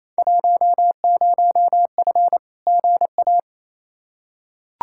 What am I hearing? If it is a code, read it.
Morse code, heard as a keyed tone, about 28 wpm, 700 Hz. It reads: 10F GA
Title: Field Day Exchanges Encore - 28wpm